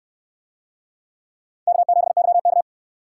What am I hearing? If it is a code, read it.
Morse code, heard as a keyed tone, about 67 wpm, 700 Hz. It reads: K7YK